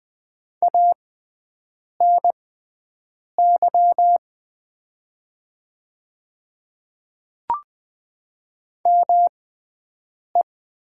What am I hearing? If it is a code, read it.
Morse code, heard as a keyed tone, about 20 wpm, spaced as for 7 wpm, 700 Hz. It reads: ANY ME
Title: 6x Character Spacing - Top 300 Words - 20wpm